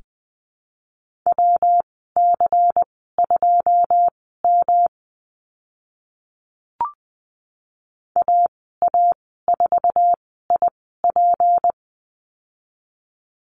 Morse code, heard as a keyed tone, about 20 wpm, spaced as for 15 wpm, 700 Hz. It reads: WC2M AA4IP
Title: 167 Lesson 27g - Number 7 - Call Signs 20wpm